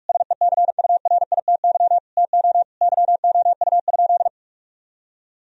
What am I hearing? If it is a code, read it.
Morse code, heard as a keyed tone, about 41 wpm, 700 Hz. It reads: SECURITY TO YOU?
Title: Sentences from Top 1000 Words - Review Part 4 - 45wpm